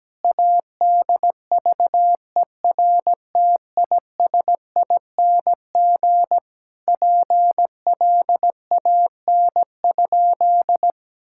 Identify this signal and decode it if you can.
Morse code, heard as a keyed tone, about 17 wpm, 700 Hz. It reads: ADVERTISING PLAN?
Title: Sentences from Top 1000 Words - Review Part 6 - 17wpm